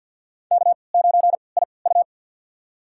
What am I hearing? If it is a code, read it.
Morse code, heard as a keyed tone, about 52 wpm, spaced as for 29 wpm, 700 Hz. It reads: K9IU
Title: USA Call Signs - 4 Characters - 3x Character Spacing - 50wpm